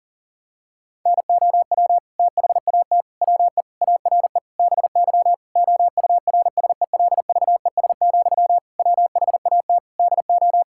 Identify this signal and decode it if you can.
Morse code, heard as a keyed tone, about 37 wpm, 700 Hz. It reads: NOW THAT WE ARE BY OURSELVES, WHAT DO
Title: Sentences from Top 1600 Words 40wpm